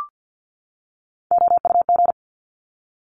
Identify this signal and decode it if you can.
Morse code, heard as a keyed tone, about 50 wpm, 700 Hz. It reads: OUG
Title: Top Three Letter Combinations in Words 50wpm